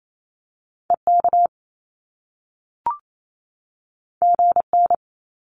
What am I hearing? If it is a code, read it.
Morse code, heard as a keyed tone, about 28 wpm, 700 Hz. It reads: EK GN
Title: Top Two Letter Combinations in Words 28wpm